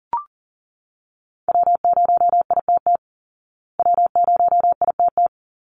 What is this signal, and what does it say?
Morse code, heard as a keyed tone, about 40 wpm, 700 Hz. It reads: W0ITT W0ITT
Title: Call Signs Repeated 3x - 40wpm